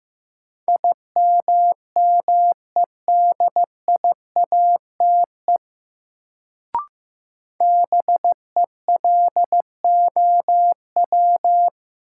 Morse code, heard as a keyed tone, about 15 wpm, 700 Hz. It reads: IMMEDIATE BELOW
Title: Sentences from Top 1200 Words - New Word Drill Alt - 15wpm